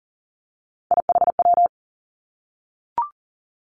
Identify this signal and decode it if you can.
Morse code, heard as a keyed tone, about 38 wpm, 690 Hz. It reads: IHW
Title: Three Challenging Letter-Number Combinations 40wpm